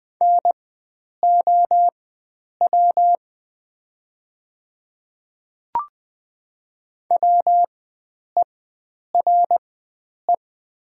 Morse code, heard as a keyed tone, about 20 wpm, spaced as for 9 wpm, 700 Hz. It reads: NOW WERE
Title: 4x Character Spacing - Top 100 Words - 20wpm